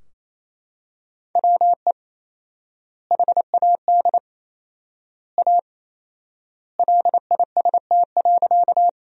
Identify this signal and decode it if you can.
Morse code, heard as a keyed tone, about 28 wpm, 700 Hz. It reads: WE HAD A LIST.
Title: 4x Word-Spacing - Sentences from Top 400 Words - 28wpm